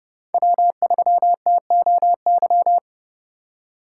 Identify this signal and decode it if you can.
Morse code, heard as a keyed tone, about 30 wpm, 700 Hz. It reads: W3TOY
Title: USA Call Signs - 5 Characters - 2x Character Spacing - 30wpm